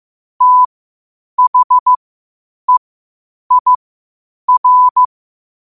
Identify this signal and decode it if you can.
Morse code, heard as a keyed tone, about 15 wpm, spaced as for 8 wpm, 1000 Hz. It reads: THEIR
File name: qotdfiles / 8wpm